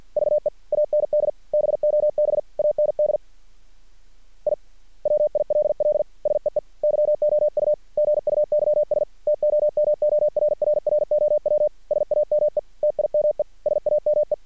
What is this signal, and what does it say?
Morse code, heard as a keyed tone, about 50 wpm, 600 Hz. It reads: VE AND BOB AND I WILL SEE YOU GUYS TOMORRROW SAME TIME SAME
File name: archived / CRSnet